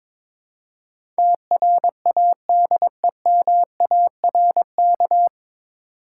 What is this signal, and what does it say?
Morse code, heard as a keyed tone, about 22 wpm, 700 Hz. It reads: TRADEMARK
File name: 9 Letter Words - First 3 Letters Given - 22wpm